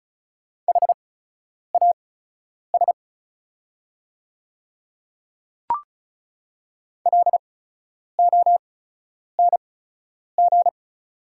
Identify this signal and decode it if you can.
Morse code, heard as a keyed tone, about 36 wpm, spaced as for 10 wpm, 690 Hz. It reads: HAS LONG
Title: 8x Character Spacing - Top 100 Words - 35wpm